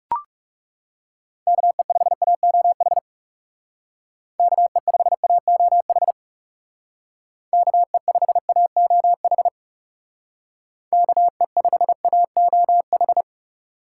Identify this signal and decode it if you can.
Morse code, heard as a keyed tone, about 40 wpm, 700 Hz. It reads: KE5AOH KE5AOH KE5AOH KE5AOH